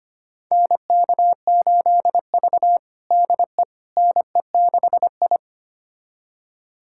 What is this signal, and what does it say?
Morse code, heard as a keyed tone, about 25 wpm, 700 Hz. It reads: NK8V DE NE6I
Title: USA Callsign DE Callsign - 4 Characters - 25wpm